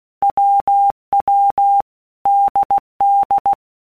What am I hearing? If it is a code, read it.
Morse code, heard as a keyed tone, about 16 wpm, 800 Hz. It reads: WW DD
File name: All Groups 1-minute Blocks / Groups@15 / AWUNDG